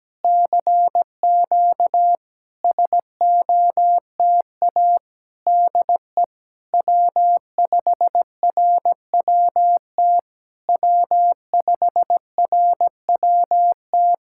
Morse code, heard as a keyed tone, about 17 wpm, 700 Hz. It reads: CQ SOTA DE W5RWT W5RWT
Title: SOTA QSO Practice 17wpm